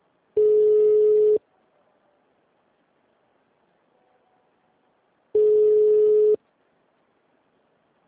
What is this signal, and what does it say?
Ringback tone (Europe, 425 Hz).